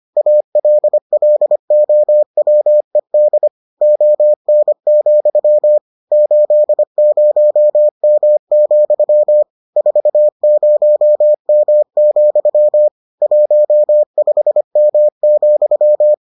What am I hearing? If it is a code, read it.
Morse code, heard as a keyed tone, about 25 wpm, 600 Hz. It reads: ALLOWED ON, 80M, 40M, 15M,